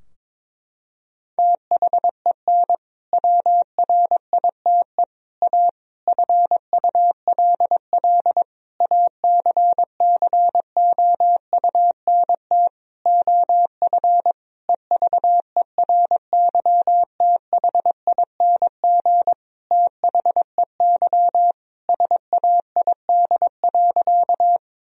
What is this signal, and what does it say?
Morse code, heard as a keyed tone, about 22 wpm, 700 Hz. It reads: THEN WRITE A FULL ACCOUNT OF EVERYTHING THEY SAID.
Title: Sentences from Top 1000 Words - Review Part 3 - 22wpm